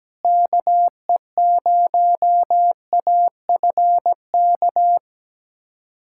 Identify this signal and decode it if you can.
Morse code, heard as a keyed tone, about 17 wpm, 700 Hz. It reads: KE0AFK
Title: USA Callsign DE Callsign 17wpm